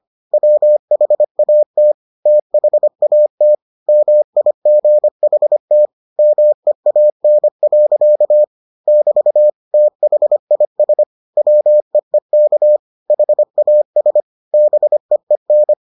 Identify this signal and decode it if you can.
Morse code, heard as a keyed tone, about 25 wpm, 600 Hz. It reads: WHAT THAT MIGHT MEAN. <BT> THIS WEEK HAS BEEN